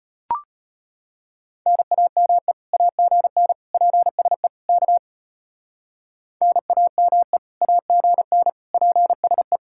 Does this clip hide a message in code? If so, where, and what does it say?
Yes: Morse code from 1.7 to 9.6 s, heard as a keyed tone, about 32 wpm, 700 Hz. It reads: NAME AGN PSE K NAME AGN PSE